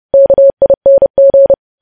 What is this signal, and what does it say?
Morse code, heard as a keyed tone, about 30 wpm, 550 Hz. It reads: KING
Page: news headlines in morse code at 30 wpm